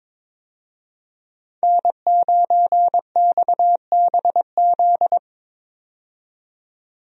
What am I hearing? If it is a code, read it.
Morse code, heard as a keyed tone, about 21 wpm, 700 Hz. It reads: N9XBZ
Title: USA Call Signs - 5 Characters - 22wpm